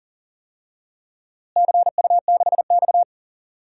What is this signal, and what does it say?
Morse code, heard as a keyed tone, about 40 wpm, 700 Hz. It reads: CU6X
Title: DXCC-100 Call Signs - 4 Characters - Repeated 2x - 40wpm